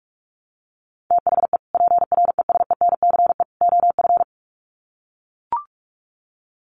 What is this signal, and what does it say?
Morse code, heard as a keyed tone, about 44 wpm, 700 Hz. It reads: THE PRESENCE OF
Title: Sets of 3 Words 45wpm